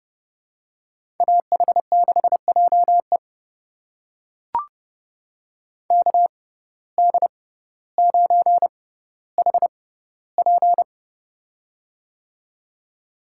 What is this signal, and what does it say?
Morse code, heard as a keyed tone, about 31 wpm, spaced as for 16 wpm, 700 Hz. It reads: AH6JE KD9HP
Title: USA Call Signs - 5 Characters - 6x Character Spacing - 30wpm